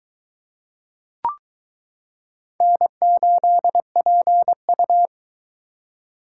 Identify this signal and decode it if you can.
Morse code, heard as a keyed tone, about 23 wpm, 700 Hz. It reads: N8PU